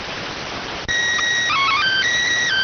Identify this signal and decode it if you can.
SSTV (slow-scan television), PD 290.